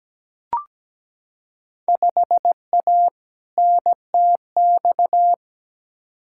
Morse code, heard as a keyed tone, about 17 wpm, 700 Hz. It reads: 5A NTX
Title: Field Day Exchanges - 17wpm